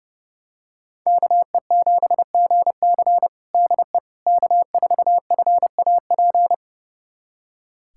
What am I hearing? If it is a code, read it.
Morse code, heard as a keyed tone, about 30 wpm, 700 Hz. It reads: KE7GC DE K4FAP
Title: USA Callsign DE Callsign 30wpm